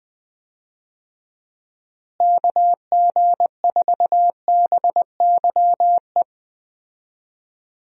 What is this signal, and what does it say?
Morse code, heard as a keyed tone, about 20 wpm, 700 Hz. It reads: KG4BYE
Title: USA Call Signs - 6 Characters - Repeated 3x - 20wpm